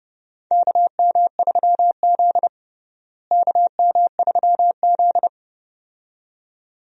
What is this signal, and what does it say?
Morse code, heard as a keyed tone, about 31 wpm, 700 Hz. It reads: KM3Z KM3Z
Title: USA Call Signs - 4 Characters - Repeated 2x - 30wpm